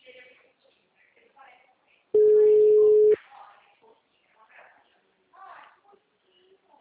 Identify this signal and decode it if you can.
Ringback tone (Europe, 425 Hz), with speech.